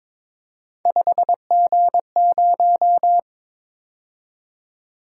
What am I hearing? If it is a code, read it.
Morse code, heard as a keyed tone, about 28 wpm, 700 Hz. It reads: 5G0